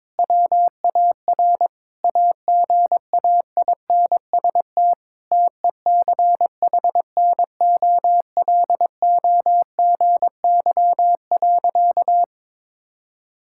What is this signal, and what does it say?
Morse code, heard as a keyed tone, about 22 wpm, 700 Hz. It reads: WAR AGAINST TECHNOLOGY.
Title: Sentences from Top 1000 Words - Review Part 5 - 22wpm